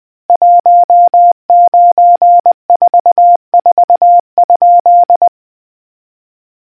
Morse code, heard as a keyed tone, about 20 wpm, 700 Hz. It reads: 1944?